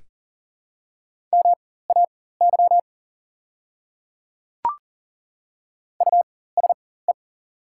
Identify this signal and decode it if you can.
Morse code, heard as a keyed tone, about 40 wpm, spaced as for 19 wpm, 700 Hz. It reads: MAY USE
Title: 4x Character Spacing - Top 300 Words - 40wpm